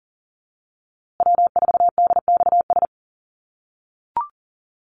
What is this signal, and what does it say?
Morse code, heard as a keyed tone, about 40 wpm, 700 Hz. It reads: W4DXS
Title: USA Call Signs Repeated 2x 40wpm